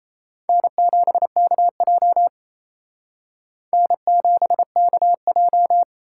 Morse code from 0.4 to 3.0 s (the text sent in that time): N7KJ